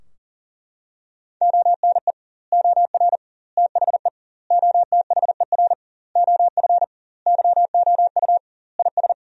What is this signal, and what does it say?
Morse code, heard as a keyed tone, about 37 wpm, 700 Hz. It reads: ONE OR THE OTHER OF YOU IS
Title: 2x Word-Spacing - Sentences from Top 400 Words - 40wpm